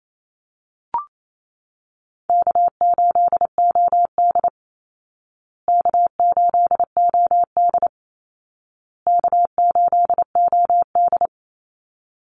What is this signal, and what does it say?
Morse code, heard as a keyed tone, about 28 wpm, 700 Hz. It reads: K8OD K8OD K8OD
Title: Easy Call Signs Repeated 3x - 28wpm